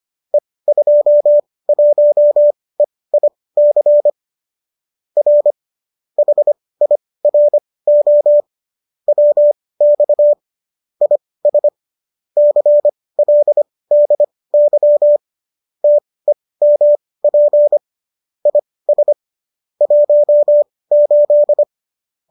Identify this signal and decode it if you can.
Morse code, heard as a keyed tone, about 25 wpm, spaced as for 18 wpm, 590 Hz. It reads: E21EIC R HIRO WX IS CLDY TEMP IS 18